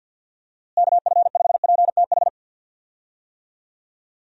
Morse code, heard as a keyed tone, about 46 wpm, 700 Hz. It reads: KF5PTH
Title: USA Call Signs - 6 Characters - Repeated 2x - 50wpm